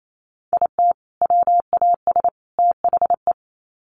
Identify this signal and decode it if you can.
Morse code, heard as a keyed tone, about 28 wpm, 700 Hz. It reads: IT WAS THE
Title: Sets of 3 Words 28wpm